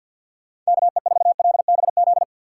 Morse code, heard as a keyed tone, about 52 wpm, 700 Hz. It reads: KE4LBZ
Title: USA Call Signs - 6 Characters - 6x Character Spacing - 50wpm